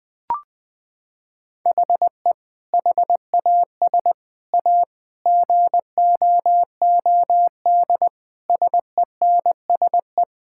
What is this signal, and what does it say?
Morse code, heard as a keyed tone, about 20 wpm, 700 Hz. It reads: HE HAS A GOOD SENSE
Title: Sentences with 2-Word Phrases - 20wpm